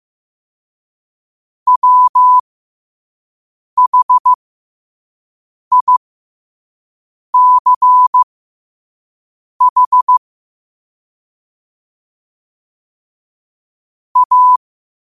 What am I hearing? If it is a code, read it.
Morse code, heard as a keyed tone, about 15 wpm, spaced as for 5 wpm, 1000 Hz. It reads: WHICH A
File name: qotdfiles / 5wpm